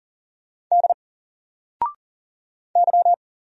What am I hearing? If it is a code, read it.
Morse code, heard as a keyed tone, about 39 wpm, 700 Hz. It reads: D Y
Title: Single Character (Letter:Number:Punctuation:Prosign) 40wpm